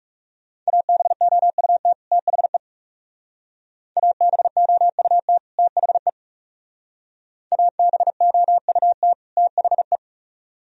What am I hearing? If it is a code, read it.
Morse code, heard as a keyed tone, about 37 wpm, 700 Hz. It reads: ABOUT THE ABOUT THE ABOUT THE
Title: Speed-Racing - Sets of 2 Words 30wpm